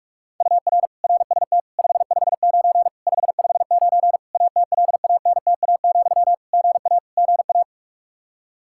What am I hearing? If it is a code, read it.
Morse code, heard as a keyed tone, about 41 wpm, 700 Hz. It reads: UR RST 559 559 ATLANTA, GA GA
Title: K1USN SST QSO Practice 45wpm